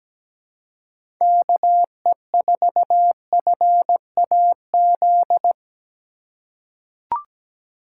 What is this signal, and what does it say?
Morse code, heard as a keyed tone, about 17 wpm, 700 Hz. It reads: KE4FAZ
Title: USA Call Signs - 2x Character Spacing - 17wpm